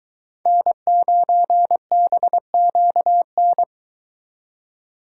Morse code, heard as a keyed tone, about 22 wpm, 700 Hz. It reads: N9BQN